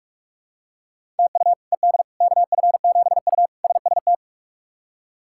Morse code, heard as a keyed tone, about 41 wpm, 700 Hz. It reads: TU ED KF7U SST
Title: K1USN SST QSO Practice - 45wpm